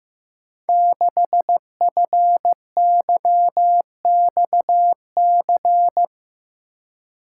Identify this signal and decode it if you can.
Morse code, heard as a keyed tone, about 15 wpm, 700 Hz. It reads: 6FYXC
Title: Code Groups - 5 Letter:Numbers - 15wpm